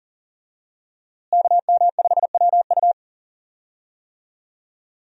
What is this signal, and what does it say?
Morse code, heard as a keyed tone, about 38 wpm, 700 Hz. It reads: KM5WU
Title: USA Call Signs - 5 Characters - Repeated 2x - 40wpm